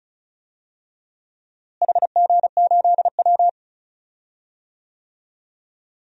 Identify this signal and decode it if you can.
Morse code, heard as a keyed tone, about 36 wpm, 700 Hz. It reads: HG8W
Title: DXCC-100 Call Signs - 4 Characters - Repeated 3x - 35wpm